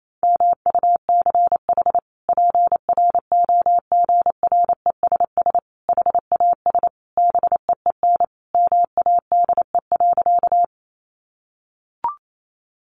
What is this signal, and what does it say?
Morse code, heard as a keyed tone, about 28 wpm, 700 Hz. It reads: MUCH PROGRESS HAS BEEN MADE.
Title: Sentences from Top 1400 Words 28wpm